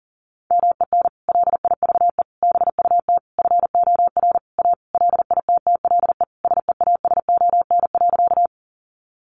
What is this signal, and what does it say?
Morse code, heard as a keyed tone, about 40 wpm, 700 Hz. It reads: MEN LIVE BUT FOR A LITTLE SEASON.
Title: Sentences from Top 1400 Words 40wpm